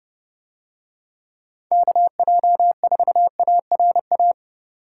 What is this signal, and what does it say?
Morse code, heard as a keyed tone, about 31 wpm, 700 Hz. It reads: KJ4ARA